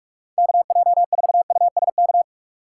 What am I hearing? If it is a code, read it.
Morse code, heard as a keyed tone, about 46 wpm, 700 Hz. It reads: KJ4USK
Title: USA Call Signs - 6 Characters - 4x Character Spacing - 45wpm